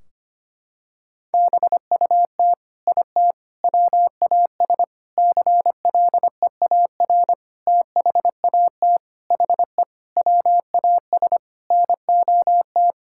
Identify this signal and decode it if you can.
Morse code, heard as a keyed tone, about 25 wpm, 700 Hz. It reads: BUT IT WAS CLEAR THAT HE WAS NOT
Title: Sentences from Top 1600 Words 25wpm